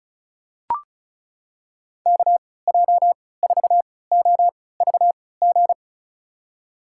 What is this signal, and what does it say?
Morse code, heard as a keyed tone, about 36 wpm, spaced as for 20 wpm, 700 Hz. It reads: KJ4OVG
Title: USA Call Signs - 6 Characters - 3x Character Spacing - 35wpm